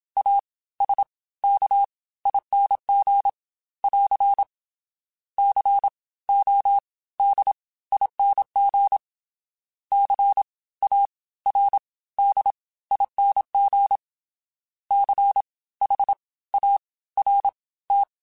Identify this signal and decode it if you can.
Morse code, heard as a keyed tone, about 27 wpm, spaced as for 17 wpm, 800 Hz. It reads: ASKING <AR> CODING CARDING CHART
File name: Special INGs / Special INGs-mp3s / Fast ing @25